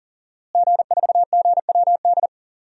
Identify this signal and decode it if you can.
Morse code, heard as a keyed tone, about 40 wpm, 700 Hz. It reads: G4GWD
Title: DXCC-100 Call Signs - 5 Characters - Repeated 2x - 40wpm